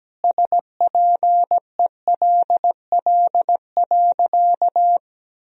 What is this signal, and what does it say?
Morse code, heard as a keyed tone, about 17 wpm, 700 Hz. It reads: SPELL.
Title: Sentences from Top 1000 Words - Review Part 1 - 17wpm